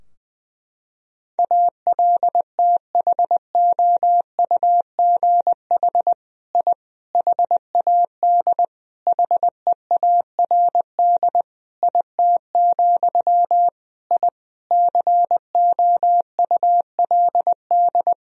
Morse code, heard as a keyed tone, about 20 wpm, 700 Hz. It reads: ALTHOUGH I HAD HEARD IT, I COULD
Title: Sentences from Top 1000 Words - Review Part 3 - 20wpm